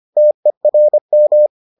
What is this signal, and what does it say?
Morse code, heard as a keyed tone, about 25 wpm, 600 Hz. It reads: TERM